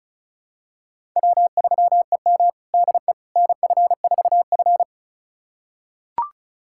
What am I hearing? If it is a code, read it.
Morse code, heard as a keyed tone, about 36 wpm, 700 Hz. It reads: W3EM DE NF4F
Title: USA Callsign DE Callsign - 4 Characters - 35wpm